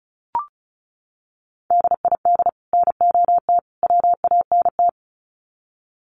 Morse code, heard as a keyed tone, about 34 wpm, 700 Hz. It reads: DID NOT WANT
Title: Sets of 3 Words 35wpm